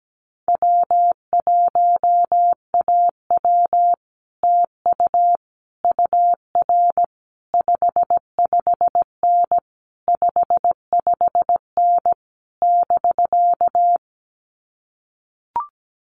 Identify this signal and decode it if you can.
Morse code, heard as a keyed tone, about 17 wpm, 700 Hz. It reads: W1AW TU UR 55N 55N <BK>
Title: Common QSO Phrases Encore 17wpm